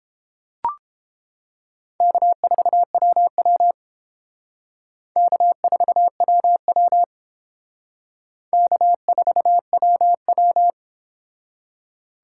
Morse code, heard as a keyed tone, about 30 wpm, 700 Hz. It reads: K4WW K4WW K4WW